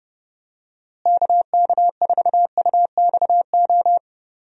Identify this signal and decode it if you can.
Morse code, heard as a keyed tone, about 30 wpm, 700 Hz. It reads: KK4UXO
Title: USA Call Signs - 6 Characters - Repeated 3x - 30wpm